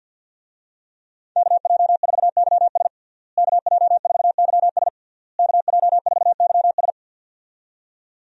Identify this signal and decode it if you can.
Morse code, heard as a keyed tone, about 50 wpm, 700 Hz. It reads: KJ4YS KJ4YS KJ4YS
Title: USA Call Signs - 5 Characters - Repeated 3x - 50wpm